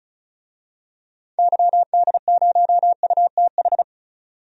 Morse code, heard as a keyed tone, about 34 wpm, 700 Hz. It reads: YD0UTH
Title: DXCC-100 Call Signs - 6 Characters - 35wpm